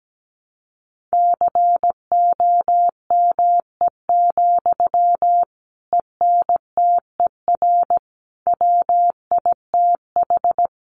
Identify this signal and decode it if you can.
Morse code, heard as a keyed tone, about 17 wpm, 700 Hz. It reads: COME, ENTER WITH
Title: Sentences from Top 1300 Words 17wpm